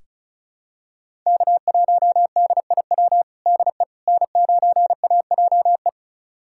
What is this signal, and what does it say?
Morse code, heard as a keyed tone, about 36 wpm, 700 Hz. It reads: K1DIW DE N9AJE
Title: USA Callsign DE Callsign - 5 Characters - 35wpm